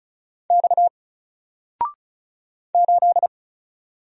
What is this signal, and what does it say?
Morse code, heard as a keyed tone, about 33 wpm, 700 Hz. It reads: X 8
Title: Single Letter-Number 35wpm